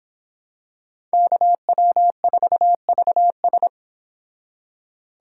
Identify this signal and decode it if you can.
Morse code, heard as a keyed tone, about 26 wpm, 700 Hz. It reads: KW4VS